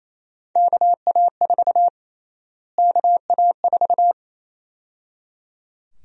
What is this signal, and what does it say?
Morse code, heard as a keyed tone, about 28 wpm, 700 Hz. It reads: KA4 KA4
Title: USA Call Sign Prefixes - Repeated 2x - 28wpm